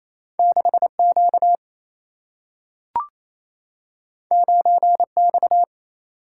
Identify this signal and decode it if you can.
Morse code, heard as a keyed tone, about 28 wpm, 700 Hz. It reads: 6 Q 9 X
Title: Single-Letter Single-Number 28wpm